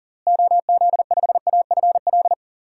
Morse code, heard as a keyed tone, about 40 wpm, 700 Hz. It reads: OZ5AFL
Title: DXCC-100 Call Signs - 6 Characters - Repeated 3x - 40wpm